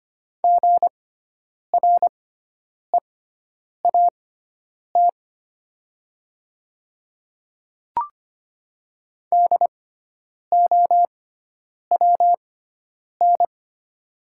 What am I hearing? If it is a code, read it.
Morse code, heard as a keyed tone, about 25 wpm, spaced as for 9 wpm, 700 Hz. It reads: GREAT DOWN
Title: 6x Character Spacing - Top 200 Words - 25wpm